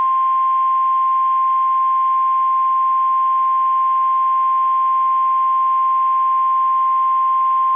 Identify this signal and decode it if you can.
Emergency Alert System (SAME) tones (the 1050 Hz attention tone). It is heard from 0.4 to 7.4 s.
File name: eas / KSDZKDJL / KSDZKDJL